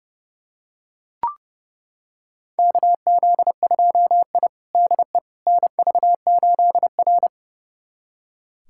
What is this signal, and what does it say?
Morse code, heard as a keyed tone, about 30 wpm, 700 Hz. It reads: KZ2I DE NV8R
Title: USA Callsign DE Callsign - 4 Characters - 30wpm